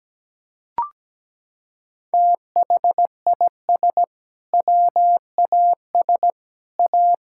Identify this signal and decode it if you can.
Morse code, heard as a keyed tone, about 17 wpm, 700 Hz. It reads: THIS WAS A
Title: Sentences from Top 1600 Words 17wpm